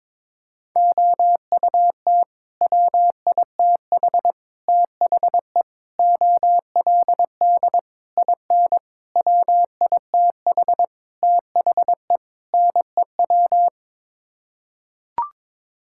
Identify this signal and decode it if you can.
Morse code, heard as a keyed tone, about 22 wpm, 700 Hz. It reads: OUT WITH THE OLD IN WITH THE NEW
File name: practice / 22wpm / Sets of 8 Words - 22wpm